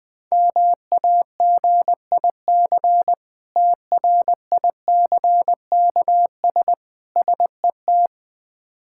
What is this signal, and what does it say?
Morse code, heard as a keyed tone, about 21 wpm, 700 Hz. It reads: MAGIC TRICKS SET
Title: Difficult Words with Hints - 20wpm